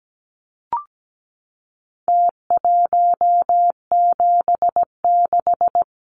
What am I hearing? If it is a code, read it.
Morse code, heard as a keyed tone, about 17 wpm, 700 Hz. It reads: T176
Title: Numbers - Contest Style - 4-Digit 17wpm